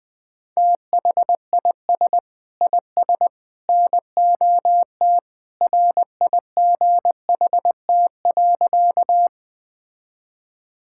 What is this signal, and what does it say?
Morse code, heard as a keyed tone, about 21 wpm, 700 Hz. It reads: THIS IS NOT RIGHT.
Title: Sentences with 3-Word Phrases - Spoken Beginning - 20wpm